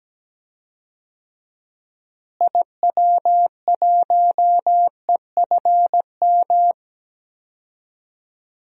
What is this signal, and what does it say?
Morse code, heard as a keyed tone, about 17 wpm, 700 Hz. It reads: IW1EFM
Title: DXCC-100 Call Signs - 6 Characters - Repeated 2x - 17wpm